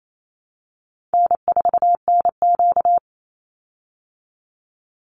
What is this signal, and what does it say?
Morse code, heard as a keyed tone, about 28 wpm, 700 Hz. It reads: N4NQ
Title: Call Signs Repeated 2x - 28wpm